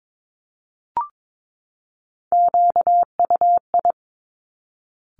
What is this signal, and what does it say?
Morse code, heard as a keyed tone, about 22 wpm, 700 Hz. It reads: QUI
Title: Top Three Letter Combinations in Words 22wpm